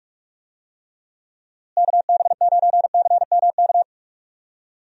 Morse code, heard as a keyed tone, about 47 wpm, 700 Hz. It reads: KB9CMK
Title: USA Call Signs - 6 Characters - Repeated 2x - 45wpm